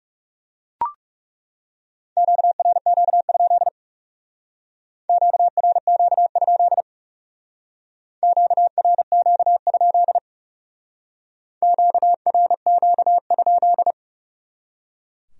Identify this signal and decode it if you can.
Morse code, heard as a keyed tone, about 36 wpm, 700 Hz. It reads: QRQ? QRQ? QRQ? QRQ?
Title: Speed-Racing Q-Codes - 30wpm